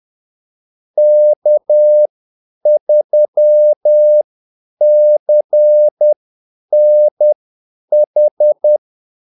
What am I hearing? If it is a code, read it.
Morse code, heard as a keyed tone, about 10 wpm, spaced as for 8 wpm, 600 Hz. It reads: K3CNH
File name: Callsigns-10-8wpm